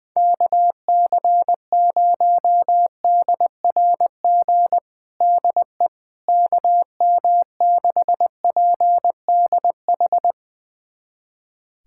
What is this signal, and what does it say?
Morse code, heard as a keyed tone, about 20 wpm, 700 Hz. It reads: KC0DRG DE KM6PDH
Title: USA Callsign DE Callsign - 6 Characters - 20wpm